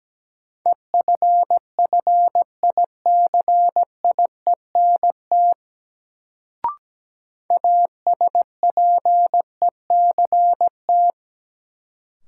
Morse code, heard as a keyed tone, about 17 wpm, 700 Hz. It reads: EFFICIENT ASPECT
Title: Sentences from Top 1300 Words - New Word Drill - 17wpm